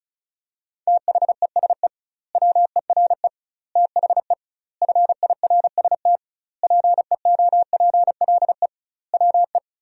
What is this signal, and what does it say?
Morse code, heard as a keyed tone, about 36 wpm, 700 Hz. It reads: THESE WERE THE FIRST PEOPLE WE
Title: 2x Word-Spacing - Sentences from Top 300 Words - 35wpm